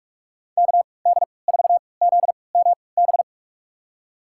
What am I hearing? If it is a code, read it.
Morse code, heard as a keyed tone, about 46 wpm, spaced as for 26 wpm, 700 Hz. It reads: KD4ZMB
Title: USA Call Signs - 6 Characters - 3x Character Spacing - 45wpm